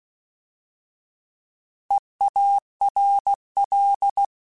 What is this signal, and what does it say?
Morse code, heard as a keyed tone, about 16 wpm, 800 Hz. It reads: EARL
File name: Alphabetical / Alphabetical mp3s / Es @15